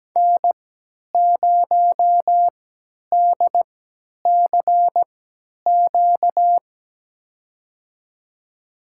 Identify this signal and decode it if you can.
Morse code, heard as a keyed tone, about 17 wpm, spaced as for 10 wpm, 700 Hz. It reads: N0DCQ
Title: USA Call Signs - 3x Character Spacing - 17wpm